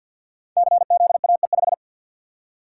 Morse code, heard as a keyed tone, about 48 wpm, 700 Hz. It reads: CZAE5
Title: Code Groups - 5 Letter:Numbers - 50wpm